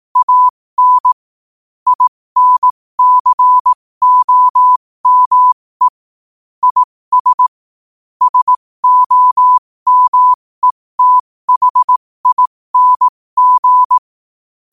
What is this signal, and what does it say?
Morse code, heard as a keyed tone, about 18 wpm, spaced as for 15 wpm, 1000 Hz. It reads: AN INCOME IS SOMETHING
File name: qotdfiles / 15wpm